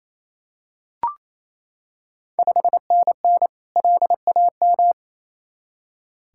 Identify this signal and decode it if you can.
Morse code, heard as a keyed tone, about 29 wpm, 690 Hz. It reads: 5NN LAM
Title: Ontario QSO Party Exchange 28wpm